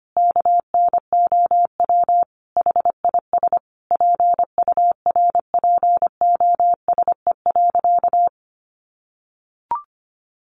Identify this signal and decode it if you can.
Morse code, heard as a keyed tone, about 25 wpm, 700 Hz. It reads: KNOW HIS PURPOSE.
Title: Sentences from Top 800 Words 25wpm